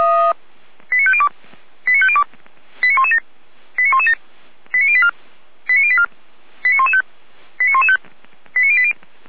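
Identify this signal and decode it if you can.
Five-tone selective call (ZVEI), heard as quick runs of five tones, tones of about 70 ms. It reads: 89502 89502 89117 89117 89084 89084 89115 89115 89080
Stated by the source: ZVEI five-tone selective call 89057 received on SDR.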